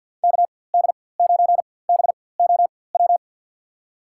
Morse code, heard as a keyed tone, about 47 wpm, spaced as for 23 wpm, 700 Hz. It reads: KD9BOW
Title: USA Call Signs - 6 Characters - 4x Character Spacing - 50wpm